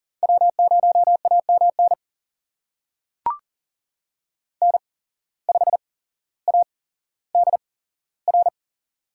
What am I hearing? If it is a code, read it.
Morse code, heard as a keyed tone, about 40 wpm, spaced as for 17 wpm, 700 Hz. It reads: W0AMN N5ADR
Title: USA Call Signs - 8x Character Spacing - 40wpm